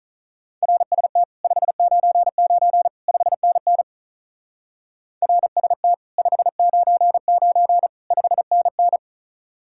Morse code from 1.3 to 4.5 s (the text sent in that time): 599 5NN